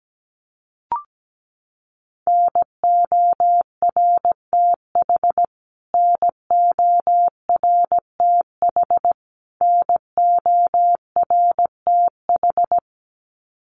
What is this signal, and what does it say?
Morse code, heard as a keyed tone, about 17 wpm, 700 Hz. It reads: NORTH NORTH NORTH
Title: Top 300 Words Repeated 17wpm